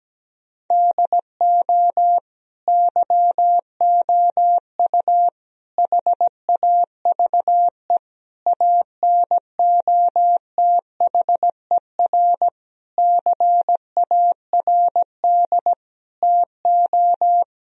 Morse code, heard as a keyed tone, about 17 wpm, 700 Hz. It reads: DO YOU HAVE ANOTHER CARD TO
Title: Sentences from Top 1000 Words - Review Part 4 - 17wpm